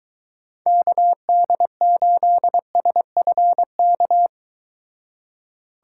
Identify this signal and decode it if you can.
Morse code, heard as a keyed tone, about 23 wpm, 700 Hz. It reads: KD8SFK